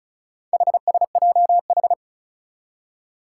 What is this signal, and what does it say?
Morse code, heard as a keyed tone, about 36 wpm, 700 Hz. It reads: HSJH
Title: Four Challenging Letter-Number Combinations 35wpm